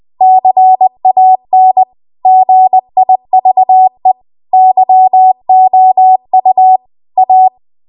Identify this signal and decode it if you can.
Morse code, heard as a keyed tone, about 20 wpm, 750 Hz. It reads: CAN GIVE YOU A